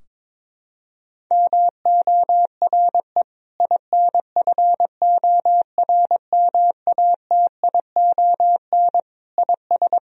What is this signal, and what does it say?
Morse code, heard as a keyed tone, about 23 wpm, 700 Hz. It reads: MORE INFORMATION IS
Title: Sentences from Top 1000 Words - Review Part 3 - 22wpm